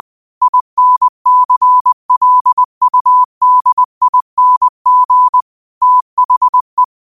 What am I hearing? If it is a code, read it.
Morse code, heard as a keyed tone, about 20 wpm, 1000 Hz. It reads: INCLUDING THE
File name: qotdfiles / 20wpm